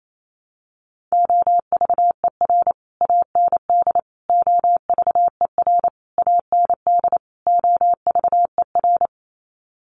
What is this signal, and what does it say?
Morse code, heard as a keyed tone, about 28 wpm, 700 Hz. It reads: OVER AND OVER AND OVER
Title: Sets of 5 Words Encore 28wpm